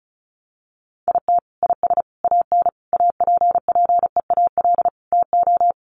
Morse code, heard as a keyed tone, about 34 wpm, 700 Hz. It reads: IT IS AN APPEAL TO
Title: Sentences from Top 1300 Words 35wpm